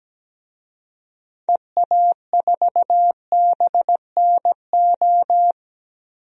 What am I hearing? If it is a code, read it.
Morse code, heard as a keyed tone, about 17 wpm, 700 Hz. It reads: EA4BNO
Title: DXCC-100 Call Signs - 6 Characters - Repeated 2x - 17wpm